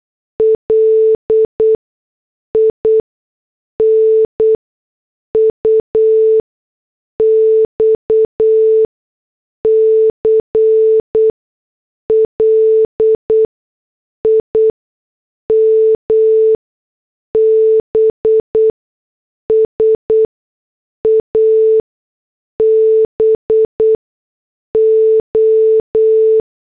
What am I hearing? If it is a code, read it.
Morse code, heard as a keyed tone, about 8 wpm, spaced as for 6 wpm, 440 Hz. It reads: LINUX CLIMBS ABO